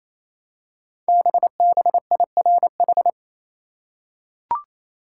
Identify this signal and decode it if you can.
Morse code, heard as a keyed tone, about 29 wpm, 700 Hz. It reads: BBIRH
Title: Code Groups - 5 Letters - 28wpm